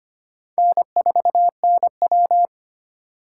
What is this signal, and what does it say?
Morse code, heard as a keyed tone, about 25 wpm, 700 Hz. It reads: N4NW
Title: K1USN SST QSO Practice - 25wpm